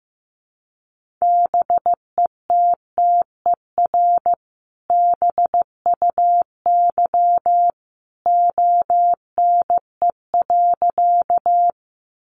Morse code, heard as a keyed tone, about 15 wpm, 700 Hz. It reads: BETTER BUY ONE.
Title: Sentences from Top 700 Words 15wpm